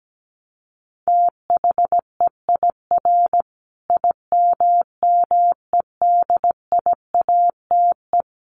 Morse code, heard as a keyed tone, about 17 wpm, 700 Hz. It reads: THEIR IMMEDIATE
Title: Sentences from Top 1200 Words 17wpm